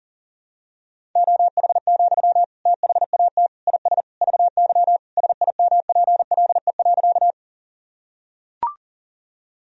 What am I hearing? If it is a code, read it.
Morse code, heard as a keyed tone, about 40 wpm, 700 Hz. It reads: OH, THAT IS VY SIMPLE.
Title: Sentences from Top 500 Words and Abbreviations 40wpm